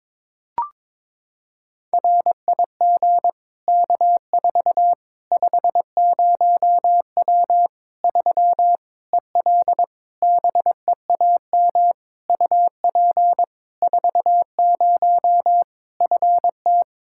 Morse code, heard as a keyed tone, about 22 wpm, 700 Hz. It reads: RIG K4 50W 3 EL BEAM UP 40 FT
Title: Practice QSO - Exchange 2 - Info - 22wpm